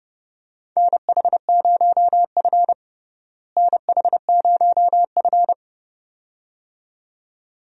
Morse code, heard as a keyed tone, about 32 wpm, 700 Hz. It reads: NH0F NH0F
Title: USA Call Signs - 4 Characters - Repeated 2x - 30wpm